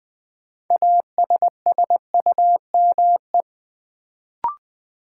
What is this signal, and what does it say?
Morse code, heard as a keyed tone, about 21 wpm, 700 Hz. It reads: ASSUME
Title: Sentences from Top 1500 Words - New Word Drill - 20wpm